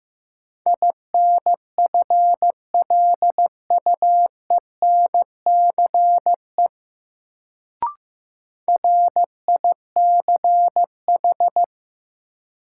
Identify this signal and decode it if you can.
Morse code, heard as a keyed tone, about 15 wpm, 700 Hz. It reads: INFLUENCE RICH
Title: Sentences from Top 1100 Words - New Word Drill - 15wpm